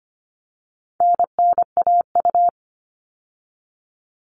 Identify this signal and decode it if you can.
Morse code, heard as a keyed tone, about 25 wpm, 700 Hz. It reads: NNAU
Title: All Cut Numbers - 4-Digit 25wpm